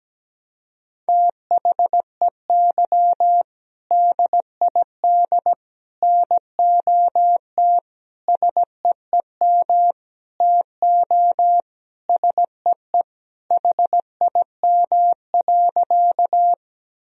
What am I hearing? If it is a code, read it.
Morse code, heard as a keyed tone, about 17 wpm, 700 Hz. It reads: THEY DID NOT SEEM TO SEE HIM.
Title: Sentences from Top 1000 Words - Review Part 1 - 17wpm